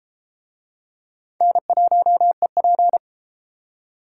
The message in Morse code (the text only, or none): N1EP